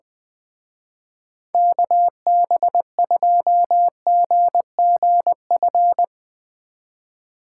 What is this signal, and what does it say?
Morse code, heard as a keyed tone, about 23 wpm, 700 Hz. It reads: KB2GGF
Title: Speed Racing - USA 2x3 Call Signs 20wpm